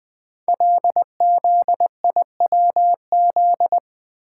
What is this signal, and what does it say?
Morse code, heard as a keyed tone, about 20 wpm, 700 Hz. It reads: LZIWZ
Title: Code Groups - 5 Letters - 20wpm